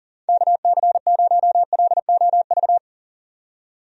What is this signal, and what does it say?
Morse code, heard as a keyed tone, about 40 wpm, 700 Hz. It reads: KC0LOV
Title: USA Call Signs - 6 Characters - 40wpm